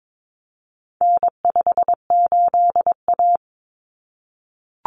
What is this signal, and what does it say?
Morse code, heard as a keyed tone, about 22 wpm, 700 Hz. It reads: N58A
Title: Common Cut Numbers - 4-Digit 22wpm